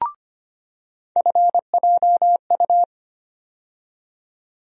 Morse code, heard as a keyed tone, about 25 wpm, 700 Hz. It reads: FJU
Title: USA Call Sign Postfixes - 3 Letters 25wpm